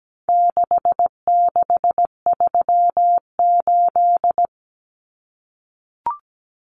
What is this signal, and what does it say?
Morse code, heard as a keyed tone, about 17 wpm, 700 Hz. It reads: 6638
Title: Numbers - 4-Digit 17wpm